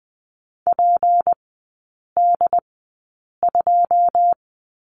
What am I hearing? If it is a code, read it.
Morse code, heard as a keyed tone, about 20 wpm, spaced as for 8 wpm, 700 Hz. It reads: PD2
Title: 159 Lesson 26e - Letter g - Three Characters 20wpm